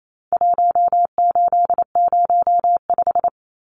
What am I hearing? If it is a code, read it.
Morse code, heard as a keyed tone, about 28 wpm, 700 Hz. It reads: 1805
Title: Numbers - Contest Style - 4-Digit 28wpm